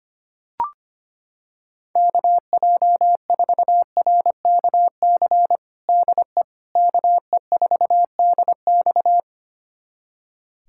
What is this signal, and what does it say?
Morse code, heard as a keyed tone, about 25 wpm, 700 Hz. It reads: KJ4RKC DE KE4DX
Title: USA Callsign DE Callsign 25wpm